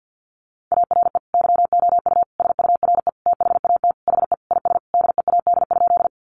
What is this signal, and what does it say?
Morse code, heard as a keyed tone, about 50 wpm, 700 Hz. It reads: ARE YOU SURE THAT HE IS DEAD?
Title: Sentences from Top 1500 Words 50wpm